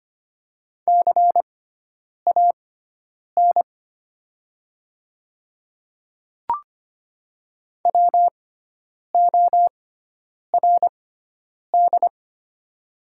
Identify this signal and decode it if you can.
Morse code, heard as a keyed tone, about 25 wpm, spaced as for 9 wpm, 700 Hz. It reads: CAN WORD
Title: 6x Character Spacing - Top 100 Words - 25wpm